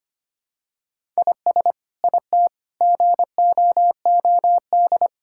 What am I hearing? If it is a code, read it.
Morse code, heard as a keyed tone, about 25 wpm, 700 Hz. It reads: IS IT GOOD
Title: Sentences from Top 1000 Words - Review Part 5 - 25wpm